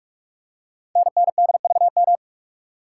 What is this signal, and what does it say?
Morse code, heard as a keyed tone, about 44 wpm, 690 Hz. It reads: NNDVM
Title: Code Groups - 5 Letters - 45wpm